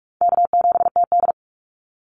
Morse code, heard as a keyed tone, about 44 wpm, 700 Hz. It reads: K7TD
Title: Easy Call Signs Repeated 3x - 45wpm